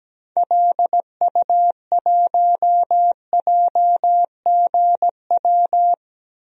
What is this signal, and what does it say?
Morse code, heard as a keyed tone, about 17 wpm, 700 Hz. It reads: LU1JGW
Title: DXCC-100 Call Signs - 6 Characters - 17wpm